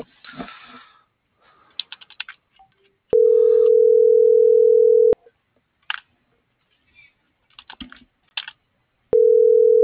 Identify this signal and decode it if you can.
Ringback tone (North America), 440 + 480 Hz.